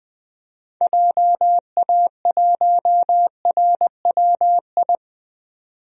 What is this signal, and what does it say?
Morse code, heard as a keyed tone, about 21 wpm, 700 Hz. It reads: JA1RWI
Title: DXCC-100 Call Signs - 6 Characters - 20wpm